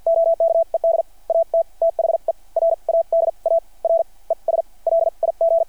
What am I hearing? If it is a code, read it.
Morse code, heard as a keyed tone, about 48 wpm, spaced as for 38 wpm, 660 Hz. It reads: OKED AT THE RADAR ES LIG
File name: archived / CRSnet